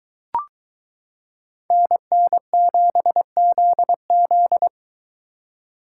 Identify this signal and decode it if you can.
Morse code, heard as a keyed tone, about 23 wpm, 700 Hz. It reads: NN7ZZ